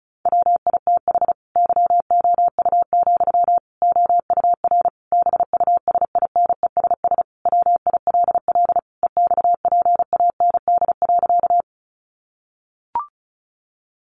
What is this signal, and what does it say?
Morse code, heard as a keyed tone, about 34 wpm, 700 Hz. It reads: WITH YOU, OUR BUSINESS WILL EXPAND.
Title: Sentences from Top 1400 Words 35wpm